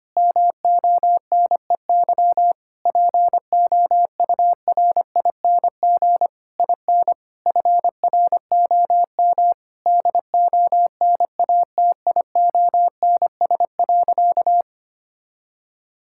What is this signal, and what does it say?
Morse code, heard as a keyed tone, about 25 wpm, 700 Hz. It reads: MONEY POURING IN FROM DONATIONS.
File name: practice / 25wpm / Sentences with 9-Word Phrases - 25wpm